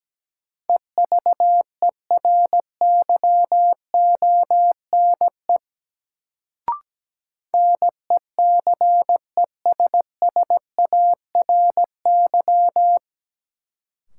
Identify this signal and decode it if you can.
Morse code, heard as a keyed tone, about 17 wpm, 700 Hz. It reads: EVERYONE NECESSARY
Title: Sentences from Top 800 Words - New Word Drill - 17wpm